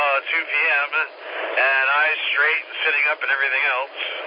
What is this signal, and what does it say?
Emergency Alert System (SAME) tones, heard as harsh buzzing bursts, with speech. It is heard from 0.3 to 3.5 s.